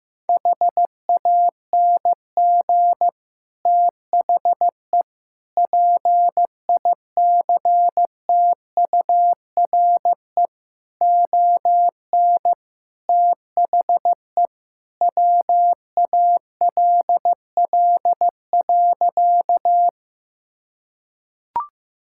Morse code, heard as a keyed tone, about 15 wpm, 700 Hz. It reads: HANG THE PICTURE ON THE WALL.
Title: Sentences from Top 1600 Words 15wpm